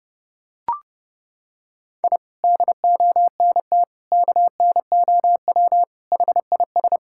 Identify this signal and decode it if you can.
Morse code, heard as a keyed tone, about 30 wpm, 700 Hz. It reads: I DONT KNOW HIS
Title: Sentences from Top 1000 Words - Review Part 6 - 30wpm